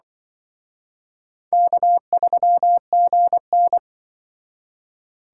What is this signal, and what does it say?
Morse code, heard as a keyed tone, about 25 wpm, 700 Hz. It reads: K3GN
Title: Speed Racing - USA 1x2 Call Signs 20wpm